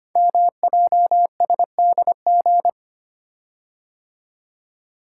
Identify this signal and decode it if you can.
Morse code, heard as a keyed tone, about 25 wpm, 700 Hz. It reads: MJSDG
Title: Code Groups - 5 Letters - 25wpm